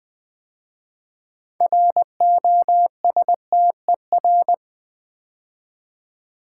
Morse code, heard as a keyed tone, about 20 wpm, 700 Hz. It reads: ROSTER
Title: 6 Letter Words - First 3 Letters Given - 20wpm